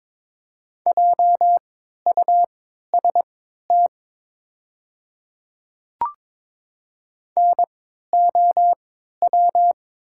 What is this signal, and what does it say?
Morse code, heard as a keyed tone, about 22 wpm, spaced as for 12 wpm, 700 Hz. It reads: JUST NOW
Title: 3x Character Spacing - Top 300 Words - 22wpm